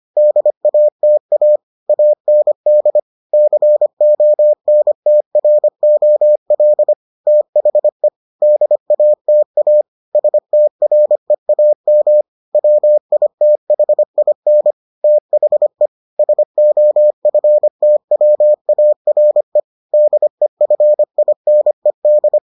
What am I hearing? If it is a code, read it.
Morse code, heard as a keyed tone, about 25 wpm, 600 Hz. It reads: DATA AND CONTROL THE DATA STREAM WITHIN THE SOFTWARE DEFINED